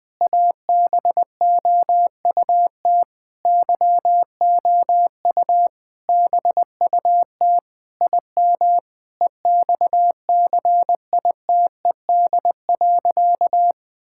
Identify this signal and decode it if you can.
Morse code, heard as a keyed tone, about 21 wpm, 700 Hz. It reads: ABOUT YOU BUT IM EXCITED.
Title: Sentences with 6-Word Phrases - Spoken Beginning - 20wpm